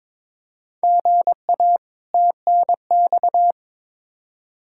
Morse code, heard as a keyed tone, about 21 wpm, 700 Hz. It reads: GA TNX
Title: Practice QSO - Exchange 1 - Beginning - 22wpm